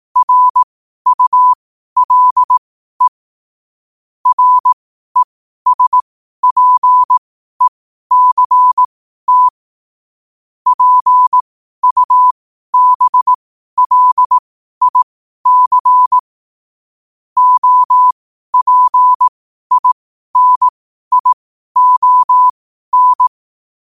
Morse code, heard as a keyed tone, about 18 wpm, spaced as for 12 wpm, 1000 Hz. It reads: RULE RESPECT PUBLIC OPINION
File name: qotdfiles / 12wpm